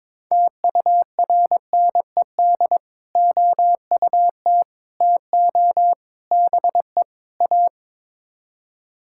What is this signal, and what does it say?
Morse code, heard as a keyed tone, about 22 wpm, 700 Hz. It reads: TURNED OUT TO BE A
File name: practice / 22wpm / Build A Phrase - 5-Word Phrases - 22wpm